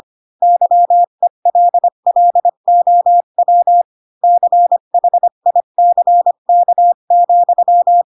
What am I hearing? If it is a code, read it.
Morse code, heard as a keyed tone, about 25 wpm, 700 Hz. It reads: YELLOW CHICK,
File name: Pinocchio / Pinocchio ch5 20 25 ik1hge